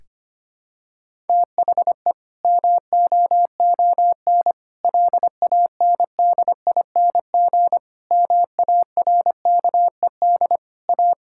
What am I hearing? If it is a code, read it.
Morse code, heard as a keyed tone, about 25 wpm, 700 Hz. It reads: THE MOON LANDING MARKED A
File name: practice / 25wpm / Difficult Sentences with Hints - 25wpm